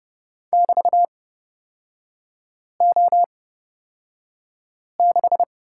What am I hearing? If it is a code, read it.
Morse code, heard as a keyed tone, about 30 wpm, spaced as for 5 wpm, 700 Hz. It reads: <BT>O6
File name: Lesson 36e - BT - Three Characters 30wpm